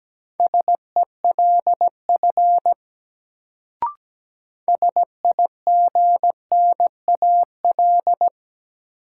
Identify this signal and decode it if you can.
Morse code, heard as a keyed tone, about 17 wpm, 700 Hz. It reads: SELF SIGNAL
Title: Sentences from Top 1400 Words - New Word Drill Alt - 17wpm